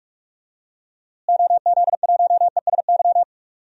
Morse code, heard as a keyed tone, about 41 wpm, 700 Hz. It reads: OZ1ESY
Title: DXCC-100 Call Signs - 6 Characters - Repeated 2x - 45wpm